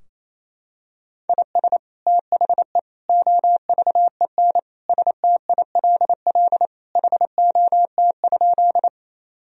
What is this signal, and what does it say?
Morse code, heard as a keyed tone, about 28 wpm, 700 Hz. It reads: IS THE OVEN STILL HOT?
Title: Sentences from Top 1000 Words - Review Part 4 - 28wpm